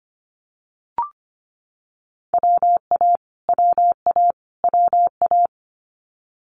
Morse code, heard as a keyed tone, about 25 wpm, 700 Hz. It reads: WA WA WA
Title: Multi-letter Phonemes Repeated 3x 25wpm